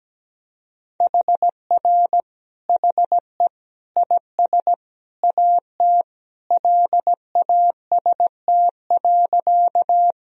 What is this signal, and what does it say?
Morse code, heard as a keyed tone, about 17 wpm, 700 Hz. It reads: HR HE IS AT LAST.
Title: Sentences from Top 300 Words and Abbreviations 17wpm